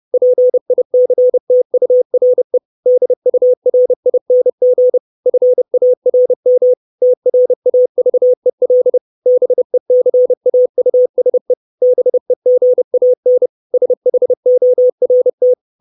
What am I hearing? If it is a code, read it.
Morse code, heard as a keyed tone, about 30 wpm, 500 Hz. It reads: PICTURE DURING FARM TRAVEL BECAUSE BEGAN SHORT
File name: 632 Words 30WPM